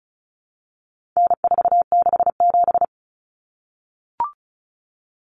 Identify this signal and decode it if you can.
Morse code, heard as a keyed tone, about 34 wpm, 700 Hz. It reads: N467
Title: Common Cut Numbers - 4-Digit 35wpm